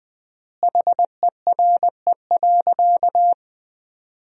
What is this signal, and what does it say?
Morse code, heard as a keyed tone, about 20 wpm, 700 Hz. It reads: HERE.
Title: 3x Word-Spacing - Sentences from Top 500 Words - 20wpm